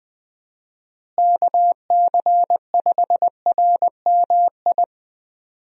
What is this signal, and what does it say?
Morse code, heard as a keyed tone, about 21 wpm, 700 Hz. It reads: KC5RMI
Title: USA Call Signs - 6 Characters - Repeated 3x - 20wpm